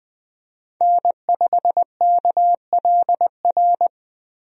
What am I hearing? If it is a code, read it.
Morse code, heard as a keyed tone, about 20 wpm, 700 Hz. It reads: N5KLR
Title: USA Call Signs - 5 Characters - Repeated 3x - 20wpm